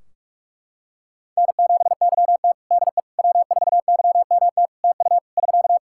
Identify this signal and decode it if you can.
Morse code, heard as a keyed tone, about 42 wpm, 700 Hz. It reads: N7YT DE W4YMT TU <SK>
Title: K1USN SST QSO Practice 45wpm